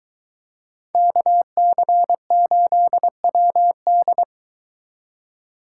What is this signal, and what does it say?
Morse code, heard as a keyed tone, about 24 wpm, 700 Hz. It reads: KC8WD